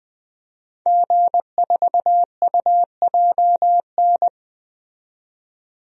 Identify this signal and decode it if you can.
Morse code, heard as a keyed tone, about 21 wpm, 700 Hz. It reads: G4UJN
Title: DXCC-100 Call Signs - 5 Characters - 20wpm